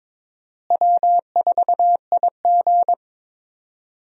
Morse code, heard as a keyed tone, about 22 wpm, 700 Hz. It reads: W4IG
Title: USA Call Signs - 4 Characters - Repeated 3x - 22wpm